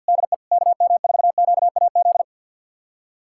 Morse code, heard as a keyed tone, about 51 wpm, 700 Hz. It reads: DE KM4QAZ
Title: USA Callsign DE Callsign - 6 Characters - 50wpm